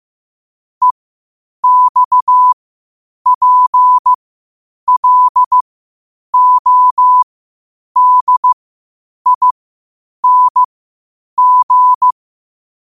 Morse code, heard as a keyed tone, about 15 wpm, spaced as for 8 wpm, 1000 Hz. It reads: EXPLODING
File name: qotdfiles / 8wpm